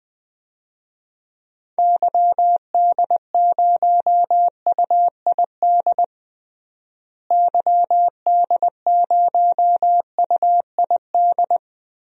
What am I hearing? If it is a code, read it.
Morse code, heard as a keyed tone, about 20 wpm, 700 Hz. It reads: YD0UID YD0UID
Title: DXCC-100 Call Signs - 6 Characters - Repeated 3x - 20wpm